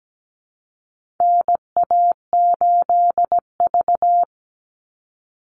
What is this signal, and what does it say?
Morse code, heard as a keyed tone, about 17 wpm, 700 Hz. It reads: NA8V
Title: Call Signs Repeated 3x - 17wpm